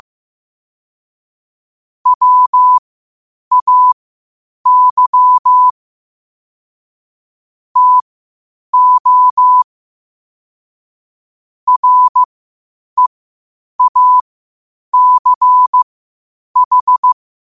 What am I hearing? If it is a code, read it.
Morse code, heard as a keyed tone, about 15 wpm, spaced as for 8 wpm, 1000 Hz. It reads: WAY TO REACH
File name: qotdfiles / 8wpm